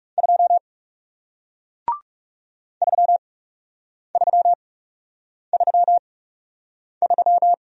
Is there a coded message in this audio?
Yes: Morse code from 0.2 to 7.5 s, heard as a keyed tone, about 38 wpm, spaced as for 8 wpm, 700 Hz. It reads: 2 3333